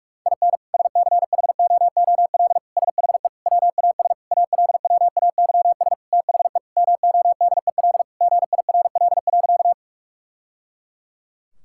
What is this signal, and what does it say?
Morse code, heard as a keyed tone, about 42 wpm, 700 Hz. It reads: IN SCHOOL SHE WAS ALWAYS THE MODEL GIRL.
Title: Sentences from Top 1000 Words - Review Part 5 - 45wpm